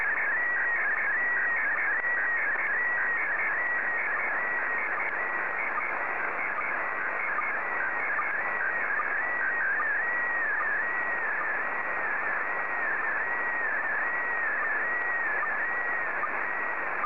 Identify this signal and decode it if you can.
SSTV (slow-scan television), PD 160.
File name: sstv / PD160-VK3DEK